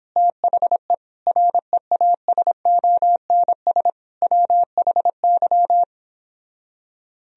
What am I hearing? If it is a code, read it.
Morse code, heard as a keyed tone, about 29 wpm, 700 Hz. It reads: THE REASONS WHY